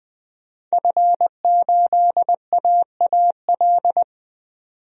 Morse code, heard as a keyed tone, about 20 wpm, 700 Hz. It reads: F8AAL
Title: DXCC-100 Call Signs - 5 Characters - 20wpm